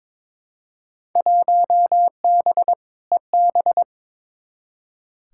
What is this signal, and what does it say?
Morse code, heard as a keyed tone, about 22 wpm, 700 Hz. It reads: 1B EB
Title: Field Day Exchanges - 22wpm